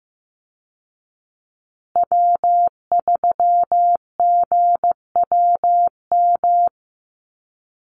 Morse code, heard as a keyed tone, about 15 wpm, 700 Hz. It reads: W3GWM
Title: Call Signs Repeated 2x - 15wpm